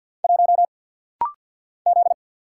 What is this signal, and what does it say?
Morse code, heard as a keyed tone, about 50 wpm, 690 Hz. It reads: 1 Z
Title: Single Letter-Number Alt - Rapid-Fire 50wpm